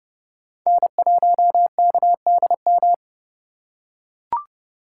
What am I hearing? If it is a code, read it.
Morse code, heard as a keyed tone, about 28 wpm, 700 Hz. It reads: N1KDM